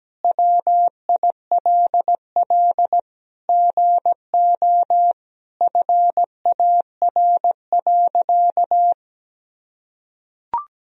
Morse code, heard as a keyed tone, about 17 wpm, 700 Hz. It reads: WILL GO FAR.
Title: Sentences from Top 400 Words and Abbreviations 17wpm